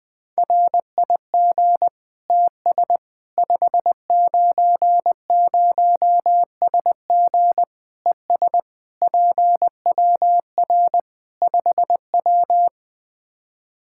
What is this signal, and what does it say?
Morse code, heard as a keyed tone, about 20 wpm, 700 Hz. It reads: RIG TS 590SG ES PWR 5W
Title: Practice QSO - Exchange 2 - Info - 20wpm